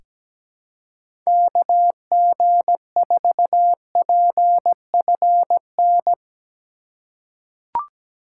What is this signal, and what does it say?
Morse code, heard as a keyed tone, about 17 wpm, 700 Hz. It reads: KG4PFN
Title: USA Call Signs - 2x Character Spacing - 17wpm